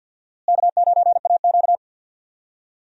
Morse code, heard as a keyed tone, about 50 wpm, 700 Hz. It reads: K9AQ
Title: USA Call Signs - 4 Characters - Repeated 2x - 50wpm